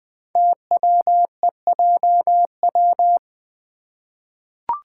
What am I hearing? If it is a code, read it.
Morse code, heard as a keyed tone, about 20 wpm, 700 Hz. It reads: TWEJW
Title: Code Groups - 5 Letters - 20wpm